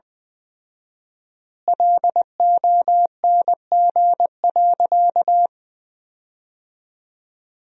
Morse code, heard as a keyed tone, about 20 wpm, 700 Hz. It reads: LONG.
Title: 4x Word-Spacing - Sentences from Top 500 Words - 20wpm